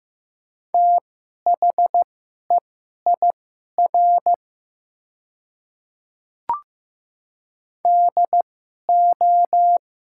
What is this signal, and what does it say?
Morse code, heard as a keyed tone, about 15 wpm, spaced as for 11 wpm, 700 Hz. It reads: THEIR DO
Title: 2x Character Spacing - Top 300 Words - 15wpm